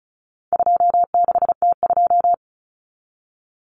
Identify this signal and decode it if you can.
Morse code, heard as a keyed tone, about 34 wpm, 700 Hz. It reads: 26T2
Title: Common Cut Numbers - 4-Digit 35wpm